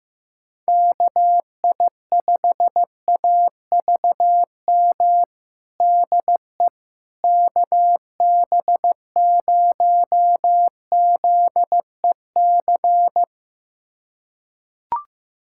Morse code, heard as a keyed tone, about 16 wpm, 700 Hz. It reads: KI5AVM DE KB0ZEC
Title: USA Callsign DE Callsign - 6 Characters - 15wpm